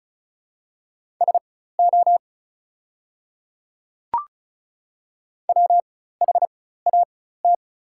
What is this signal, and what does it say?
Morse code, heard as a keyed tone, about 36 wpm, spaced as for 16 wpm, 700 Hz. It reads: SO WHAT
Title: 4x Character Spacing - Top 500 Words - 35wpm